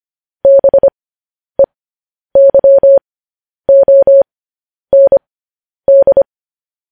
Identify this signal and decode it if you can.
Morse code, heard as a keyed tone, about 25 wpm, spaced as for 10 wpm, 550 Hz. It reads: BEYOND
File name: podcast / News headlines-1x-25x10